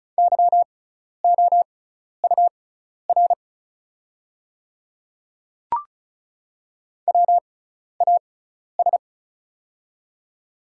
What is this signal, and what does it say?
Morse code, heard as a keyed tone, about 36 wpm, spaced as for 12 wpm, 700 Hz. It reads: YOUR WAS
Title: 6x Character Spacing - Top 100 Words - 35wpm